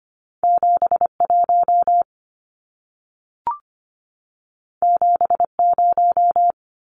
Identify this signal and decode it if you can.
Morse code, heard as a keyed tone, about 25 wpm, 700 Hz. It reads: 71 70
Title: Numbers - 2-Digit 25wpm